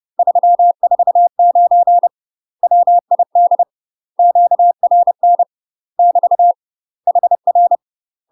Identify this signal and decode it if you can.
Morse code, heard as a keyed tone, about 30 wpm, 700 Hz. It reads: 349 WID QRN <BT> HR
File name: UK MASTER QSOs TRAINING / 30WPM / UK MASTER QSOs 0028